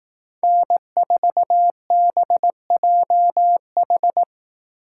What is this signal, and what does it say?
Morse code, heard as a keyed tone, about 18 wpm, 700 Hz. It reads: N4BJH
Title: Speed Racing - USA 1x3 Call Signs 15wpm